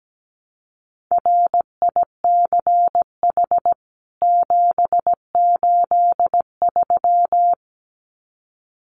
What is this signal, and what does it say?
Morse code, heard as a keyed tone, about 17 wpm, 700 Hz. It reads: RICH 783
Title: CWT Exchange 17wpm